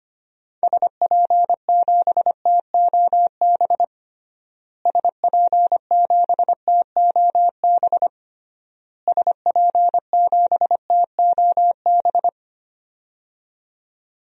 Morse code, heard as a keyed tone, about 25 wpm, 700 Hz. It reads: SP7TOB SP7TOB SP7TOB
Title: DXCC-100 Call Signs - 6 Characters - Repeated 3x - 25wpm